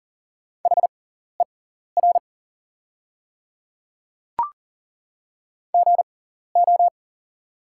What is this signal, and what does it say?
Morse code, heard as a keyed tone, about 40 wpm, spaced as for 14 wpm, 690 Hz. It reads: HER GO
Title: 6x Character Spacing - Top 100 Words - 40wpm